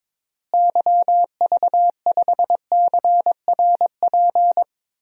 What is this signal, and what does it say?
Morse code, heard as a keyed tone, about 21 wpm, 700 Hz. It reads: YV5CRP
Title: DXCC-100 Call Signs - 6 Characters - Repeated 3x - 22wpm